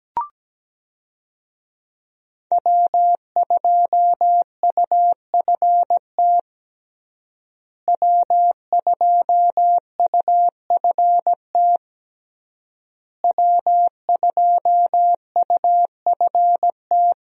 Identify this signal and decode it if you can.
Morse code, heard as a keyed tone, about 17 wpm, 700 Hz. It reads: W2UFT W2UFT W2UFT
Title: USA Call Signs - 5 Characters - Repeated 3x - 17wpm